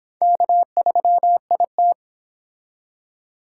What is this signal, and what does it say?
Morse code, heard as a keyed tone, about 25 wpm, 700 Hz. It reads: K3IT